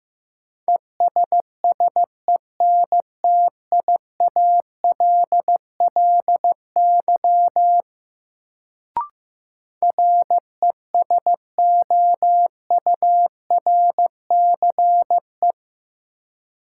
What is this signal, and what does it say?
Morse code, heard as a keyed tone, about 15 wpm, 700 Hz. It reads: ESSENTIALLY RESOURCE
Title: Sentences from Top 1300 Words - New Word Drill - 15wpm